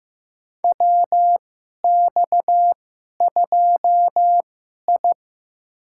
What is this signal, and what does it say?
Morse code, heard as a keyed tone, about 15 wpm, spaced as for 11 wpm, 700 Hz. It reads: WX2I
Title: USA Call Signs - 4 Characters - 2x Character Spacing - 15wpm